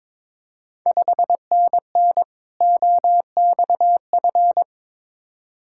Morse code, heard as a keyed tone, about 22 wpm, 700 Hz. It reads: 5NN OXF
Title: Ontario QSO Party Exchange 22wpm